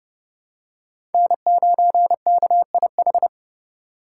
Morse code, heard as a keyed tone, about 29 wpm, 700 Hz. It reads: N9KIH
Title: USA Call Signs - 5 Characters - 2x Character Spacing - 30wpm